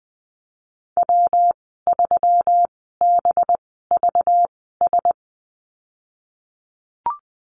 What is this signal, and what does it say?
Morse code, heard as a keyed tone, about 20 wpm, spaced as for 15 wpm, 700 Hz. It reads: W3BVS
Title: 193 Lesson 31g - Letter v - Call Signs 20wpm